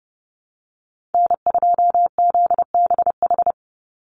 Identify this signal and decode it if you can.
Morse code, heard as a keyed tone, about 30 wpm, 700 Hz. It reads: N2ZBH
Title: USA Call Signs 30wpm